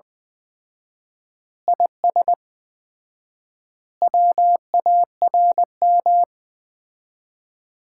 Morse code, heard as a keyed tone, about 20 wpm, 700 Hz. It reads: IS WARM
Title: Common QSO Phrases - 4x Word-Spacing - 20wpm